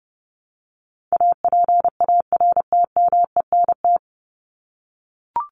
Morse code, heard as a keyed tone, about 29 wpm, 700 Hz. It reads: APARTMENT
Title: Top 751-1000 Words 30wpm